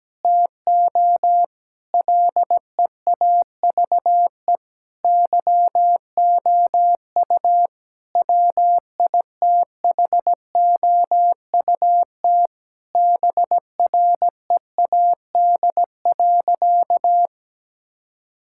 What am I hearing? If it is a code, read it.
Morse code, heard as a keyed tone, about 17 wpm, 700 Hz. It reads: TO LEAVE YOU WITHOUT BREAD.
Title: Sentences from Top 1600 Words 17wpm